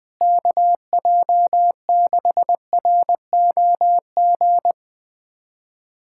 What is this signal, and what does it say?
Morse code, heard as a keyed tone, about 20 wpm, 700 Hz. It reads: KJ6ROG
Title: USA Callsign DE Callsign - 6 Characters - 20wpm